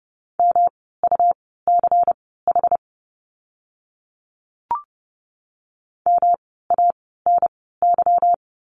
Morse code, heard as a keyed tone, about 30 wpm, spaced as for 17 wpm, 700 Hz. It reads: MUCH MANY
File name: course / 30:10wpm / 230 Lesson 37f - Letter x - Words 30wpm